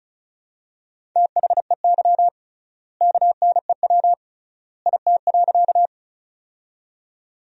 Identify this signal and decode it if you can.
Morse code, heard as a keyed tone, about 34 wpm, 700 Hz. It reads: THEY KNEW IT.
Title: 3x Word-Spacing - Sentences from Top 400 Words - 35wpm